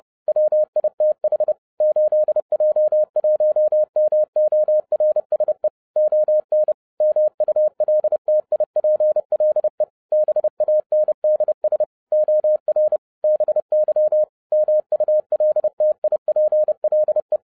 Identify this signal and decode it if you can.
Morse code, heard as a keyed tone, about 30 wpm, 610 Hz. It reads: WITH 8J1MORSE ON MULTIPLE BANDS OR BY MULTIPLE